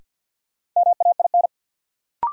Morse code, heard as a keyed tone, about 52 wpm, 700 Hz. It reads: MAIN
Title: Sentences from Top 1000 Words - Review Part 2 - New Word Drill - 50wpm